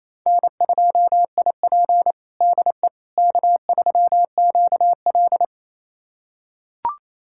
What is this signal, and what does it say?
Morse code, heard as a keyed tone, about 28 wpm, 700 Hz. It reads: N2IP DE K3QL
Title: USA Callsign DE Callsign - 4 Characters - 28wpm